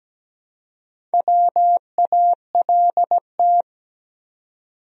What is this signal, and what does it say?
Morse code, heard as a keyed tone, about 17 wpm, 700 Hz. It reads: WALT
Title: USA Common Names - Max 4 Letters - 17wpm